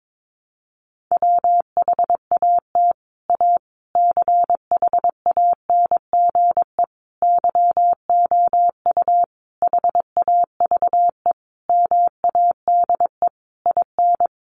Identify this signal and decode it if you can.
Morse code, heard as a keyed tone, about 22 wpm, 700 Hz. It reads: WHAT A CHANGE YOU HAVE MADE IN
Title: Sentences from Top 600 Words 22wpm